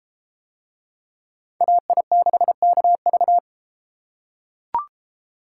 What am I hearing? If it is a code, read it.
Morse code, heard as a keyed tone, about 31 wpm, 700 Hz. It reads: AI6KV